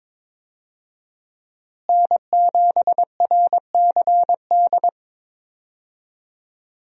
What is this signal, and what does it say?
Morse code, heard as a keyed tone, about 22 wpm, 700 Hz. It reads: N7RCD
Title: USA Call Signs - 5 Characters - Repeated 3x - 22wpm